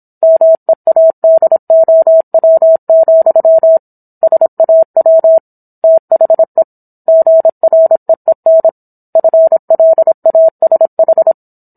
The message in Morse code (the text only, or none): MEADOW, SAW THE GREEN FLASH